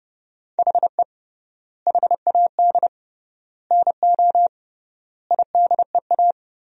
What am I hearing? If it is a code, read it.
Morse code, heard as a keyed tone, about 30 wpm, 700 Hz. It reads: HE HAD NO IDEA
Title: 3x Word-Spacing - Sentences from Top 300 Words - 30wpm